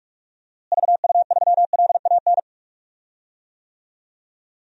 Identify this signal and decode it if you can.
Morse code, heard as a keyed tone, about 45 wpm, 700 Hz. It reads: VU3LAN
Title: DXCC-100 Call Signs - 6 Characters - Repeated 3x - 45wpm